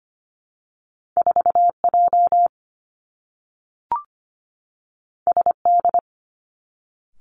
Morse code, heard as a keyed tone, about 25 wpm, 700 Hz. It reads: 4 J S D
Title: Two Challenging Letter-Number Combinations 25wpm